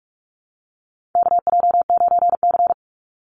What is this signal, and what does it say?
Morse code, heard as a keyed tone, about 44 wpm, 700 Hz. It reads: KJ9C
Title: Call Signs - 45wpm